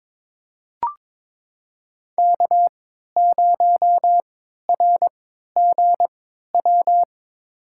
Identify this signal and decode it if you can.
Morse code, heard as a keyed tone, about 22 wpm, spaced as for 12 wpm, 700 Hz. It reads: K0RGW
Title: USA Call Signs - 5 Characters - 3x Character Spacing - 22wpm